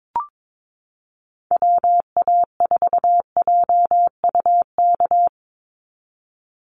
Morse code, heard as a keyed tone, about 22 wpm, 700 Hz. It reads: WA4JUK
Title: Call Signs Repeated 3x - 22wpm